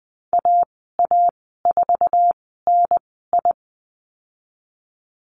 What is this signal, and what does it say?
Morse code, heard as a keyed tone, about 20 wpm, spaced as for 15 wpm, 700 Hz. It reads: AA4NI
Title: 076 Lesson 13g - Letter L - Call Signs 20wpm